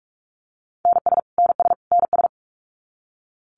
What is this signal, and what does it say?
Morse code, heard as a keyed tone, about 44 wpm, 700 Hz. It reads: NS NS NS
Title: Canadian Provinces and Territories - Repeated 3x - 45wpm